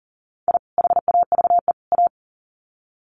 Morse code, heard as a keyed tone, about 40 wpm, 700 Hz. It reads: I HAVE A
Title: Sets of 3 Words 40wpm